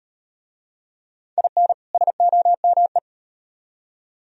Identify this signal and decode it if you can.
Morse code, heard as a keyed tone, about 35 wpm, 700 Hz. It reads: IN SOME